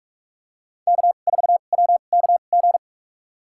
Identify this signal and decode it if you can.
Morse code, heard as a keyed tone, about 44 wpm, spaced as for 32 wpm, 700 Hz. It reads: K4WKG
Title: USA Call Signs - 5 Characters - 2x Character Spacing - 45wpm